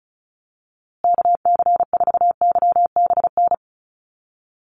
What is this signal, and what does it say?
Morse code, heard as a keyed tone, about 34 wpm, 700 Hz. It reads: KC4YBN
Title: USA Call Signs Repeated 3x 35wpm